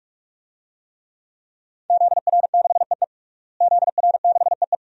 Morse code, heard as a keyed tone, about 42 wpm, 700 Hz. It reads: ZR6EE ZR6EE
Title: DXCC-100 Call Signs - 5 Characters - Repeated 2x - 45wpm